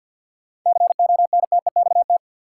Morse code, heard as a keyed tone, about 53 wpm, 700 Hz. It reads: CONTEXT
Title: Sentences from Top 1000 Words - Review Part 7 - New Word Drill - 50wpm